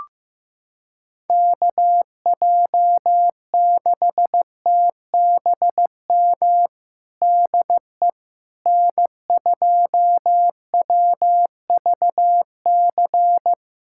Morse code, heard as a keyed tone, about 15 wpm, 700 Hz. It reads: KJ6TBM DE N2WVC
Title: USA Callsign DE Callsign 15wpm